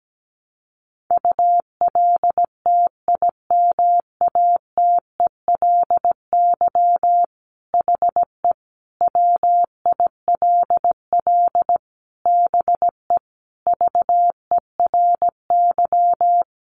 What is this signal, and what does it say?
Morse code, heard as a keyed tone, about 17 wpm, 700 Hz. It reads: ULTIMATELY HE WILL BE VERY
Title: Sentences from Top 1200 Words 17wpm